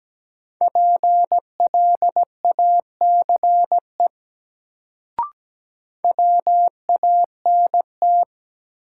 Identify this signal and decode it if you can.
Morse code, heard as a keyed tone, about 17 wpm, 700 Hz. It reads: PLACE WANT
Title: Sentences from Top 200 Words - New Word Drill - 17wpm